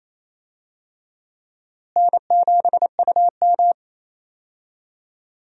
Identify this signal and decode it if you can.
Morse code, heard as a keyed tone, about 28 wpm, 700 Hz. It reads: N7UM
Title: USA Call Signs - 4 Characters - Repeated 3x - 28wpm